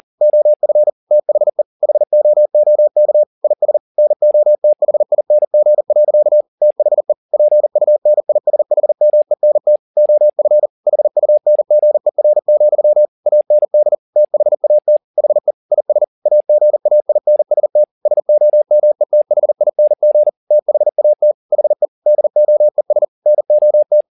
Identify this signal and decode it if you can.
Morse code, heard as a keyed tone, about 40 wpm, 600 Hz. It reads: OF THE HOOK IS NOTHING. THE PUNISHMENT OF HUNGER, AND THAT HE IS AGAINST SOMETHING THAT HE DOES NOT
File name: the old man and the sea 2 40wpm